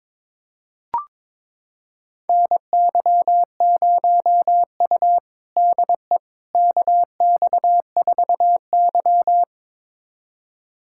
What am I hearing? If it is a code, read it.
Morse code, heard as a keyed tone, about 22 wpm, 700 Hz. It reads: NY0U DE KX4Y
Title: USA Callsign DE Callsign - 4 Characters - 22wpm